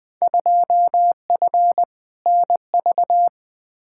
Morse code, heard as a keyed tone, about 20 wpm, 700 Hz. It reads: 2F NV
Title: Field Day Exchanges - 20wpm